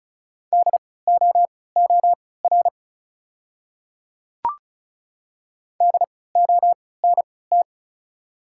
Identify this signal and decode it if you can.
Morse code, heard as a keyed tone, about 36 wpm, spaced as for 20 wpm, 700 Hz. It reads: DOOR DONT
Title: 3x Character Spacing - Top 300 Words - 35wpm